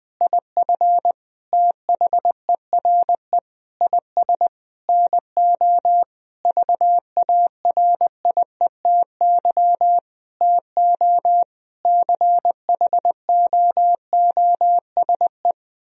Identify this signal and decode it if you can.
Morse code, heard as a keyed tone, about 20 wpm, 700 Hz. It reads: IF THERE IS NO VARIETY TO CHOOSE
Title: Sentences from Top 1000 Words - Review Part 4 - 20wpm